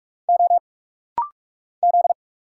Morse code, heard as a keyed tone, about 46 wpm, 690 Hz. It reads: O Z
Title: Single Letters - Rapid-Fire 45wpm